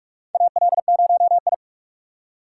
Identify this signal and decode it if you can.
Morse code, heard as a keyed tone, about 43 wpm, 700 Hz. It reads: AL0I
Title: USA Call Signs - 3x Character Spacing - 45wpm